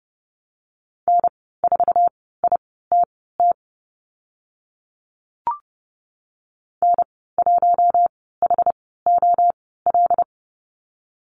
Morse code, heard as a keyed tone, about 30 wpm, spaced as for 17 wpm, 700 Hz. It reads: N4ITT N1HOL
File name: Lesson 14g - Number 2 - Call Signs 30wpm